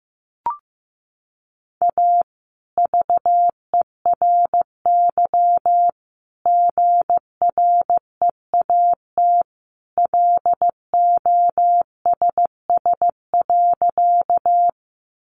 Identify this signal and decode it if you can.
Morse code, heard as a keyed tone, about 15 wpm, 700 Hz. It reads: A VERY GREAT LOSS.
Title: Sentences from Top 1400 Words 15wpm